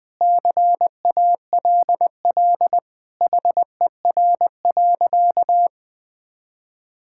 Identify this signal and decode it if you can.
Morse code, heard as a keyed tone, about 20 wpm, 700 Hz. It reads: CALL HER.
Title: Sentences with 4-Word Phrases - Spoken Beginning - 20wpm